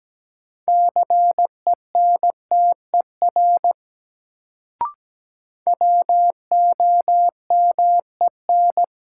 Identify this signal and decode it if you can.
Morse code, heard as a keyed tone, about 17 wpm, 700 Hz. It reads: CENTER WOMEN
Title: Sets of 3 Words - New Word Drill - 17wpm